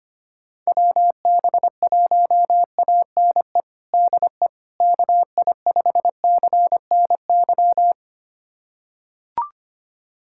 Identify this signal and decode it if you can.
Morse code, heard as a keyed tone, about 25 wpm, 700 Hz. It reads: WB1ANE DE KI5CNY
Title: USA Callsign DE Callsign - 6 Characters - 25wpm